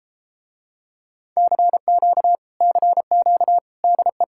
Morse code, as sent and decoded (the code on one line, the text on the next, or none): -.-. --.- / -.-. --.- / -.. .
CQ CQ DE